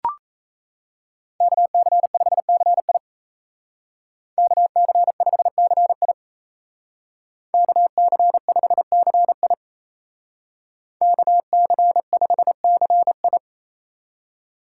Morse code, heard as a keyed tone, about 36 wpm, 700 Hz. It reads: KC5CI KC5CI KC5CI KC5CI